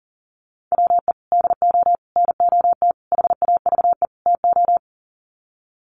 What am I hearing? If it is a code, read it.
Morse code, heard as a keyed tone, about 40 wpm, 700 Hz. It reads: WE DO NOT HAVE TO
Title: Sets of 5 Words 40wpm